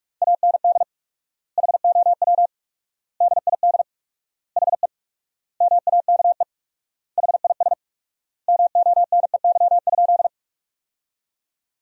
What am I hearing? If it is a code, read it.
Morse code, heard as a keyed tone, about 46 wpm, 700 Hz. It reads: AND HOW DID HE MAKE HIS MONEY?
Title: 4x Word-Spacing - Sentences from Top 400 Words - 45wpm